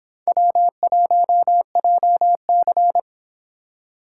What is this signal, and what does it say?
Morse code, heard as a keyed tone, about 26 wpm, 700 Hz. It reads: W1JC